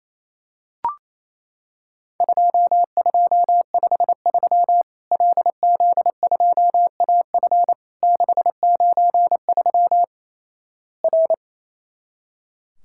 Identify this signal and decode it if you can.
Morse code, heard as a keyed tone, about 28 wpm, 700 Hz. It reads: 2253 LZ2AF 693 R
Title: Worked All Europe DX Contest - QTC Exchange - 28wpm